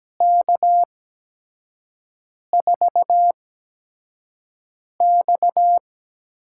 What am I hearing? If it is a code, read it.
Morse code, heard as a keyed tone, about 17 wpm, spaced as for 5 wpm, 700 Hz. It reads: K4X
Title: USA Call Signs - 5 Characters - 8x Character Spacing - 17wpm